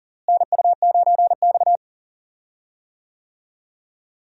Morse code, heard as a keyed tone, about 39 wpm, 700 Hz. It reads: NU9X
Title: USA Call Signs - 4 Characters - Repeated 3x - 40wpm